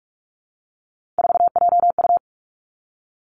Morse code, heard as a keyed tone, about 46 wpm, 700 Hz. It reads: 4JU
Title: Three Challenging Letter-Number Combinations 45wpm